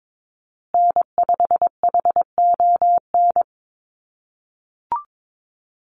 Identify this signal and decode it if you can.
Morse code, heard as a keyed tone, about 22 wpm, 700 Hz. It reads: N5HON
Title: USA Call Signs Repeated 2x 22wpm